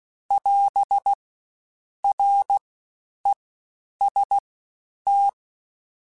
Morse code, heard as a keyed tone, about 16 wpm, 800 Hz. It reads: <AS> R E S T
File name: Combinations / Combinations-mp3s / Combinations@15